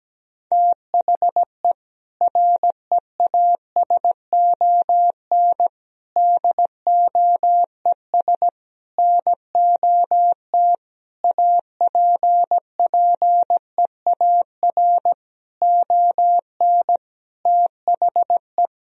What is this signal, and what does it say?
Morse code, heard as a keyed tone, about 17 wpm, 700 Hz. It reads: THE REASON DOES NOT APPEAR ON THE
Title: Sentences from Top 1000 Words - Review Part 3 - 17wpm